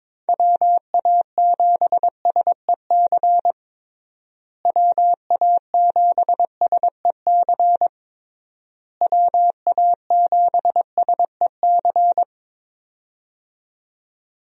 Morse code, heard as a keyed tone, about 22 wpm, 700 Hz. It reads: WA7SEC WA7SEC WA7SEC
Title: USA Call Signs - 6 Characters - Repeated 3x - 22wpm